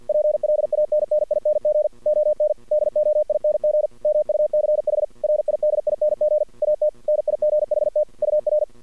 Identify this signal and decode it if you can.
Morse code, heard as a keyed tone, about 53 wpm, 600 Hz. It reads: PLANNING ON DOING MUCH RIDING AT NIGHT UR
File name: archived / CRSnet